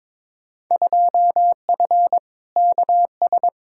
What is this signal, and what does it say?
Morse code, heard as a keyed tone, about 22 wpm, 700 Hz. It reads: 2F KS
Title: Field Day Exchanges Encore - 22wpm